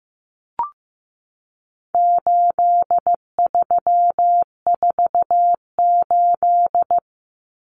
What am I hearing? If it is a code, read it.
Morse code, heard as a keyed tone, about 15 wpm, 700 Hz. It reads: 8348
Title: Common Cut Numbers - 4-Digit 15wpm